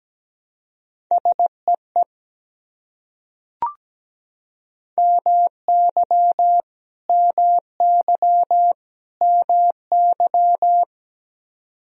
Morse code, heard as a keyed tone, about 17 wpm, 700 Hz. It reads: SEE MY MY MY
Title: Top 100 Words Repeated 17wpm